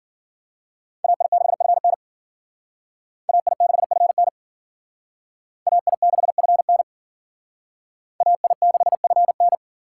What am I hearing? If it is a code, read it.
Morse code, heard as a keyed tone, about 51 wpm, 700 Hz. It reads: AI6FN AI6FN AI6FN AI6FN